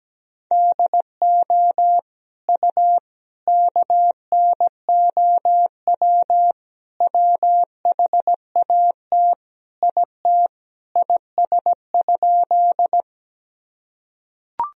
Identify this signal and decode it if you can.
Morse code, heard as a keyed tone, about 17 wpm, 700 Hz. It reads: DO U KNOW WHAT IT IS?
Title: Sentences from Top 100 Words and Abbreviations 17wpm